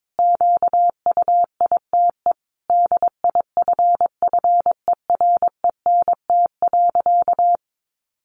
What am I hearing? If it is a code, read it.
Morse code, heard as a keyed tone, about 22 wpm, 700 Hz. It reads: QUITE DIFFERENT.
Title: Sentences from Top 700 Words 22wpm